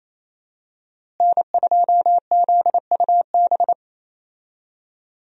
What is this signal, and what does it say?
Morse code, heard as a keyed tone, about 27 wpm, 700 Hz. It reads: N2ZUB
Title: USA Call Signs - 5 Characters - Repeated 3x - 28wpm